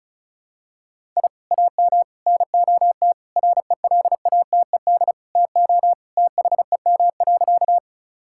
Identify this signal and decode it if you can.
Morse code, heard as a keyed tone, about 36 wpm, 700 Hz. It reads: I AM NOT RELATED TO THEM.
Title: Sentences from Top 1000 Words - Review Part 5 - 35wpm